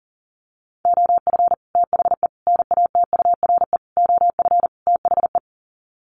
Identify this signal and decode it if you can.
Morse code, heard as a keyed tone, about 40 wpm, 700 Hz. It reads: OF THE NATURE OF THE
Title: Sets of 5 Words Encore2 40wpm